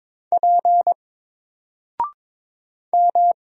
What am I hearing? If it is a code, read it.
Morse code, heard as a keyed tone, about 22 wpm, 700 Hz. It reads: P M
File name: Single-Letter-Number-22wpm